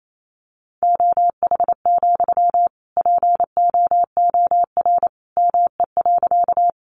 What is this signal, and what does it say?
Morse code, heard as a keyed tone, about 28 wpm, 700 Hz. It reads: OH, POOR ME.
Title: Sentences from Top 1000 Words 28wpm